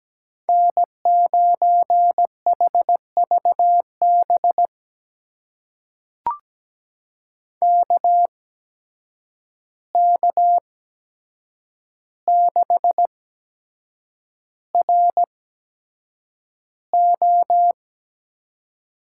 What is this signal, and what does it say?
Morse code, heard as a keyed tone, about 17 wpm, spaced as for 7 wpm, 700 Hz. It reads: N9HVB KK6RO
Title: USA Call Signs - 5 Characters - 8x Character Spacing - 17wpm